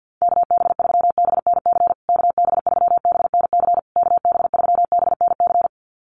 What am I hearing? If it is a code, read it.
Morse code, heard as a keyed tone, about 50 wpm, 700 Hz. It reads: KB3BNC KB3BNC KB3BNC
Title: USA Call Signs Repeated 3x 50wpm